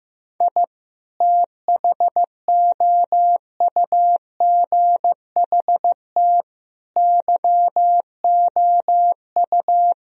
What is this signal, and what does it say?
Morse code, heard as a keyed tone, about 15 wpm, 700 Hz. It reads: I THOUGHT YOU
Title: Sentences from Top 1000 Words - Review Part 4 - 15wpm